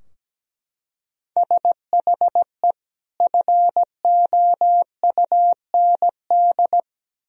Morse code, heard as a keyed tone, about 17 wpm, 700 Hz. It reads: SHE FOUND
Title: Sentences from Top 1000 Words - Review Part 3 - 17wpm